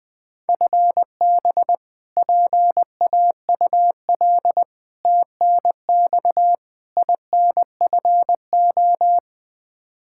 Morse code, heard as a keyed tone, about 20 wpm, 700 Hz. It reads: FB PAUL TNX INFO
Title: Practice QSO - Exchange 3 - More Info - 20wpm